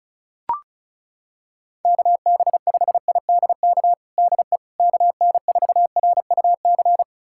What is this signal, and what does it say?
Morse code, heard as a keyed tone, about 36 wpm, 700 Hz. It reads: KB5IDK DE KN4RUC
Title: USA Callsign DE Callsign - 6 Characters - 35wpm